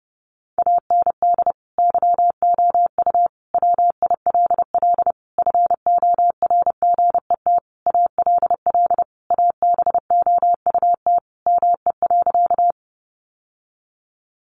Morse code, heard as a keyed tone, about 30 wpm, 700 Hz. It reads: AND YOU WILL FORGET ALL ABOUT ME.
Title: Sentences from Top 1200 Words 30wpm